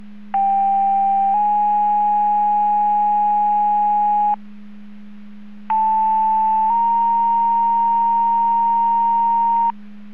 Two-tone page, heard as a one-second tone, then a three-second tone, about 790 then 830 Hz, about 880 then 930 Hz. It is heard from 0.3 to 9.7 s.
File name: Portage 154.385 Paging